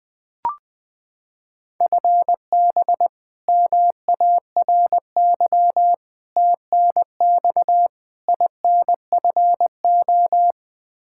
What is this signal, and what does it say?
Morse code, heard as a keyed tone, about 21 wpm, 700 Hz. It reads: FB MARY TNX INFO
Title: Practice QSO - Exchange 3 - More Info - 20wpm